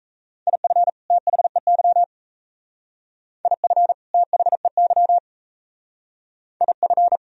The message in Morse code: IF THEY IF THEY IF

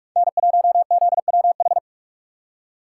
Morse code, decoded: N1ZWH

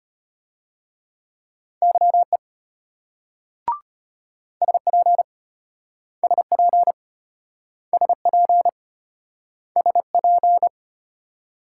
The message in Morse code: YE SP SP SP SP

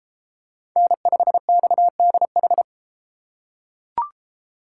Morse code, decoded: N5XDH

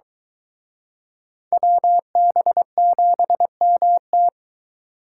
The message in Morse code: WB7MT

